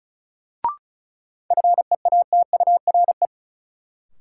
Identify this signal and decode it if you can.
Morse code, heard as a keyed tone, about 34 wpm, 700 Hz. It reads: FEATURE
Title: Sentences from Top 800 Words - New Word Drill - 35wpm